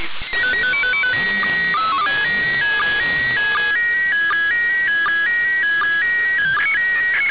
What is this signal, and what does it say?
SSTV (slow-scan television), PD 180.
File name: suws-websdr recording t18-13-19z -8khz